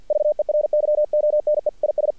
Morse code, heard as a keyed tone, about 48 wpm, 600 Hz. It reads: VERYONE IS
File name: archived / CRSnet